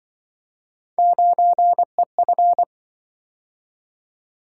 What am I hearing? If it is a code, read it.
Morse code, heard as a keyed tone, about 25 wpm, 700 Hz. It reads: 9EF
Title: Speed-Racing - Three Letter-Number Combinations - 20wpm